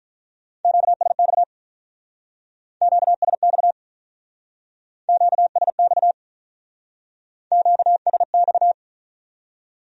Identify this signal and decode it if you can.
Morse code, heard as a keyed tone, about 45 wpm, 700 Hz. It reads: QSX QSX QSX QSX